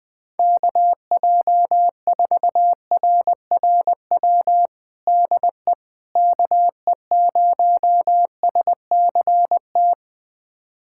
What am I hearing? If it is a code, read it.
Morse code, heard as a keyed tone, about 20 wpm, 700 Hz. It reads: KJ4RRW DE KE0SCT
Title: USA Callsign DE Callsign - 6 Characters - 20wpm